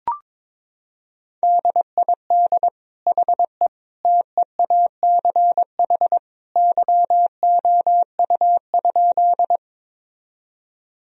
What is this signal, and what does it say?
Morse code, heard as a keyed tone, about 21 wpm, 700 Hz. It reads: DID HE TEACH YOU?
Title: Sentences from Top 1000 Words - Review Part 2 - 22wpm